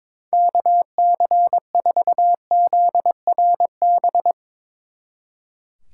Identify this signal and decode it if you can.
Morse code, heard as a keyed tone, about 24 wpm, 700 Hz. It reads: KC4ZRB
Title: Speed Racing - USA 2x3 Call Signs 22wpm